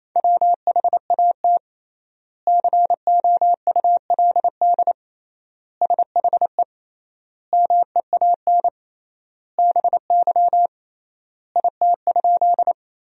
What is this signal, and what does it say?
Morse code, heard as a keyed tone, about 28 wpm, 700 Hz. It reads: WHAT COULD SHE MEAN BY IT?
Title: 3x Word-Spacing - Sentences from Top 200 Words - 28wpm